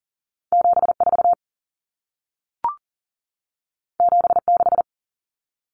Morse code, heard as a keyed tone, about 40 wpm, 700 Hz. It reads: 74 76
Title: Common Cut Numbers - 2-Digit 40wpm